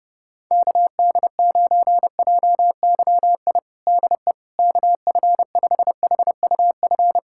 Morse code, heard as a keyed tone, about 32 wpm, 700 Hz. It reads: KD9JYI DE KF5HUF
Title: USA Callsign DE Callsign 30wpm